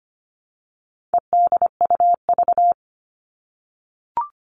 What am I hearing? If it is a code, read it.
Morse code, heard as a keyed tone, about 25 wpm, 700 Hz. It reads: EDUV
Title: All Cut Numbers - 4-Digit 25wpm